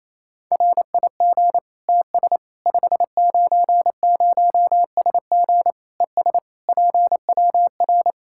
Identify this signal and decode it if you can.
Morse code, heard as a keyed tone, about 28 wpm, 700 Hz. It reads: RIG TS 590SG ES PWR
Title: Practice QSO - Exchange 2 - Info - 28wpm